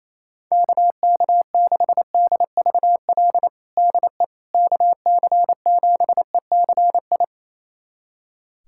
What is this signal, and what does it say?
Morse code, heard as a keyed tone, about 28 wpm, 700 Hz. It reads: KK6DVL DE KC7ECI
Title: USA Callsign DE Callsign - 6 Characters - 28wpm